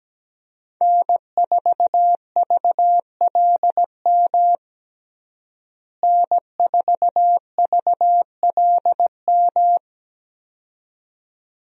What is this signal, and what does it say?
Morse code, heard as a keyed tone, about 17 wpm, 700 Hz. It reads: N4VLM N4VLM
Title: USA Call Signs - 5 Characters - Repeated 2x - 17wpm